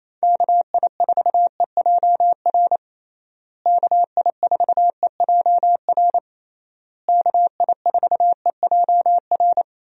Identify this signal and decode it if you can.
Morse code, heard as a keyed tone, about 28 wpm, 700 Hz. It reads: KI4EJR KI4EJR KI4EJR
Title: USA Call Signs - 6 Characters - Repeated 3x - 28wpm